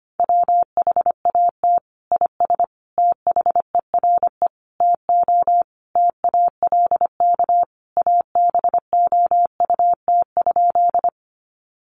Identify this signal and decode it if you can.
Morse code, heard as a keyed tone, about 25 wpm, 700 Hz. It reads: WHAT IS THERE TO TALK ABOUT?
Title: Sentences from Top 400 Words 25wpm